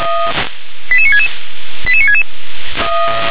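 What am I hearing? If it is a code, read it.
Five-tone selective call (ZVEI), heard as quick runs of five tones, tones of about 70 ms. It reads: 88966 88966
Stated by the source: Five-tone ZVEI selective call received by SDR.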